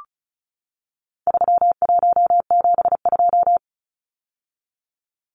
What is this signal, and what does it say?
Morse code, heard as a keyed tone, about 34 wpm, 700 Hz. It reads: 3172
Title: Numbers - 4-Digit 35wpm